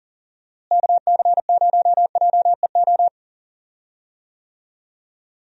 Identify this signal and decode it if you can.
Morse code, heard as a keyed tone, about 40 wpm, 700 Hz. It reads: KC0JEO
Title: USA Call Signs - 6 Characters - Repeated 3x - 40wpm